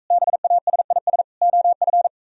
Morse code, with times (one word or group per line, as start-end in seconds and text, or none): 0.1-1.2 BASIS
1.4-2.1 OF